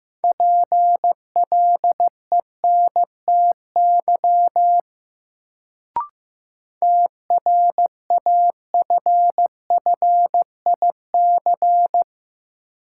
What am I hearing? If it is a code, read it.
Morse code, heard as a keyed tone, about 15 wpm, 700 Hz. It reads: PLENTY TRAFFIC
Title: Sentences from Top 1300 Words - New Word Drill - 15wpm